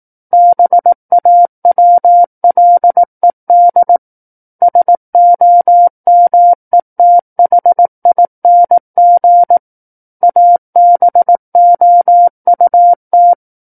Morse code, heard as a keyed tone, about 18 wpm, 700 Hz. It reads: BAWLED SOMETHING ABOUT
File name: 18 WPM